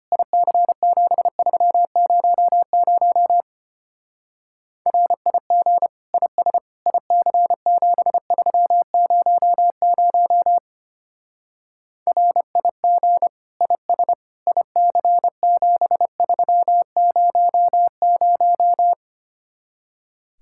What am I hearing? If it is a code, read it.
Morse code, heard as a keyed tone, about 29 wpm, 700 Hz. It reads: IC7300 RIG IS IC7300 RIG IS IC7300
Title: Speed-Racing Common QSO Phrases Encore - 25wpm